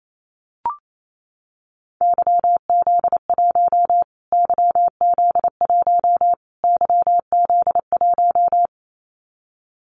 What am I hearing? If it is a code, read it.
Morse code, heard as a keyed tone, about 28 wpm, 700 Hz. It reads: YZ1 YZ1 YZ1
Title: Callsign Prefixes for Top 25 DXCC - Repeated 3x - 28wpm